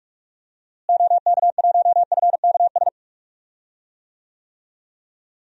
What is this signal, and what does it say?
Morse code, heard as a keyed tone, about 46 wpm, 700 Hz. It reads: OK1FKS
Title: DXCC-100 Call Signs - 6 Characters - Repeated 2x - 45wpm